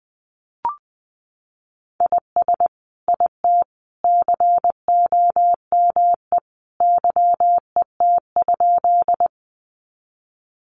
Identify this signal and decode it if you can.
Morse code, heard as a keyed tone, about 20 wpm, 700 Hz. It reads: IS IT COME YET?
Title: Sentences from Top 500 Words 20wpm